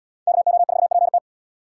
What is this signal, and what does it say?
Morse code, heard as a keyed tone, about 76 wpm, 700 Hz. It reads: KO4PT